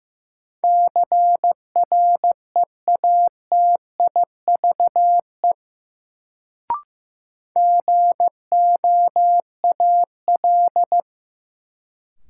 Morse code, heard as a keyed tone, about 15 wpm, 700 Hz. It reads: CREATIVE GOAL
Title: Sentences from Top 1000 Words - Review Part 5 - New Word Drill - 15wpm